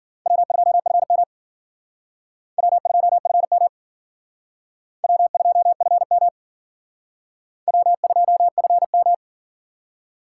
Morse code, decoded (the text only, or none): W2FM W2FM W2FM W2FM